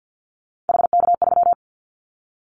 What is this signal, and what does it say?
Morse code, heard as a keyed tone, about 50 wpm, 690 Hz. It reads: HK3
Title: Callsign Prefixes for Top 100 DXCC - 50wpm